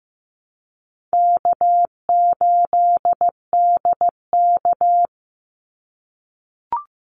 Morse code, heard as a keyed tone, about 15 wpm, 700 Hz. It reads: K8DK
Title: Easy Call Signs - 15wpm